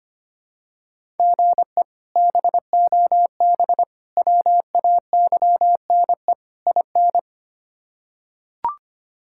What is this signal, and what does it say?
Morse code, heard as a keyed tone, about 25 wpm, 700 Hz. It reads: GE BOB WAYNE IN
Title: K1USN SST QSO Practice - 25wpm